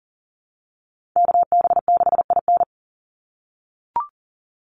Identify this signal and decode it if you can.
Morse code, heard as a keyed tone, about 40 wpm, 700 Hz. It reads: KB6IN
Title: USA Call Signs 40wpm